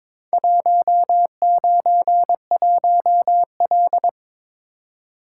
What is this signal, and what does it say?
Morse code, heard as a keyed tone, about 22 wpm, 700 Hz. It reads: 191L
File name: practice / 22wpm / Four Letter-Number Combinations - 22wpm